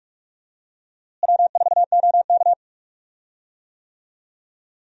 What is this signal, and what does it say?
Morse code, heard as a keyed tone, about 46 wpm, 700 Hz. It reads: W4OK
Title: USA Call Signs - 4 Characters - Repeated 3x - 45wpm